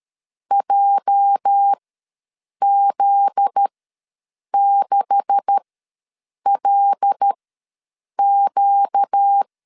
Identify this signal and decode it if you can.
Morse code, heard as a keyed tone, about 13 wpm, spaced as for 7 wpm, 800 Hz. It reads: JZ6LQ
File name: LEZIONI / 9 286 ws 12wpm 3351